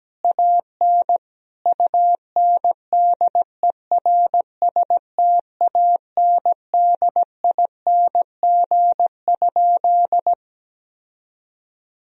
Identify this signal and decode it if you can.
Morse code, heard as a keyed tone, about 17 wpm, 700 Hz. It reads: AN UNDERSTANDING?
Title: Sentences from Top 1000 Words - Review Part 3 - 17wpm